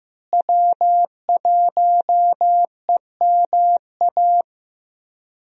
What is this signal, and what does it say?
Morse code, heard as a keyed tone, about 15 wpm, 700 Hz. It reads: W1EMA
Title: USA Call Signs - 5 Characters - Repeated 3x - 15wpm